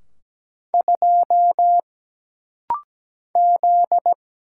Morse code, heard as a keyed tone, about 17 wpm, 700 Hz. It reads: 2 Z
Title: Single Letter-Number Alt - Rapid-Fire 17wpm